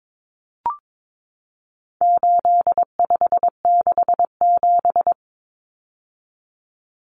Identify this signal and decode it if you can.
Morse code, heard as a keyed tone, about 22 wpm, 700 Hz. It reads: 8567
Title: Numbers - 4-Digit 22wpm